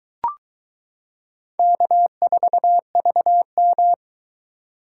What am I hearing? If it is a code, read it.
Morse code, heard as a keyed tone, about 23 wpm, 700 Hz. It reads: K4VM